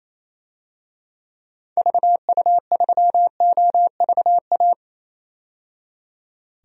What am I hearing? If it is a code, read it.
Morse code, heard as a keyed tone, about 28 wpm, 700 Hz. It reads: VU3OVA
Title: DXCC-100 Call Signs - 6 Characters - 28wpm